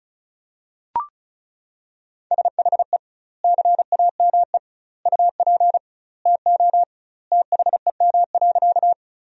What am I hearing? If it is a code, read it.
Morse code, heard as a keyed tone, about 36 wpm, 700 Hz. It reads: SHE CAME UP TO THEM.
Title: 2x Word-Spacing - Sentences from Top 200 Words - 35wpm